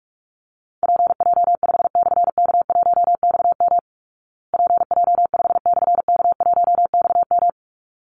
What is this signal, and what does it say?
Morse code, heard as a keyed tone, about 44 wpm, 700 Hz. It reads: PJ5/K1XM PJ5/K1XM
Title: Call Signs Repeated 3x - 45wpm